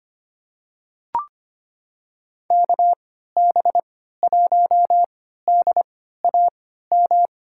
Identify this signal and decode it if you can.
Morse code, heard as a keyed tone, about 25 wpm, spaced as for 14 wpm, 700 Hz. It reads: KB1DAM
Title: USA Call Signs - 6 Characters - 3x Character Spacing - 25wpm